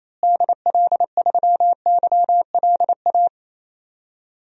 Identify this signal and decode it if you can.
Morse code, heard as a keyed tone, about 28 wpm, 700 Hz. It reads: DL3YLA
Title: DXCC-100 Call Signs - 6 Characters - Repeated 2x - 28wpm